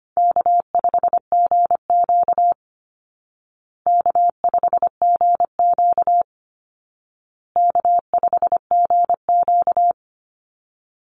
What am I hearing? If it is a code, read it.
Morse code, heard as a keyed tone, about 25 wpm, 700 Hz. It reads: K5GQ K5GQ K5GQ
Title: Easy Call Signs Repeated 3x - 25wpm